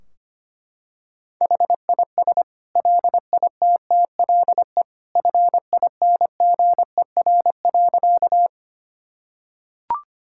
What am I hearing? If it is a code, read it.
Morse code, heard as a keyed tone, about 25 wpm, 700 Hz. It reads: HIS LITTLE FINGER.
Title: Sentences from Top 1600 Words 25wpm